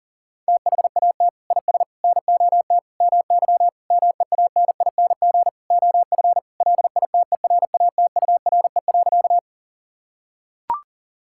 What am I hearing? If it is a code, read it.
Morse code, heard as a keyed tone, about 40 wpm, 700 Hz. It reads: THAT IS NOT MY MEANING OF LITERATURE.
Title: Sentences from Top 1000 Words - Review Part 6 - 40wpm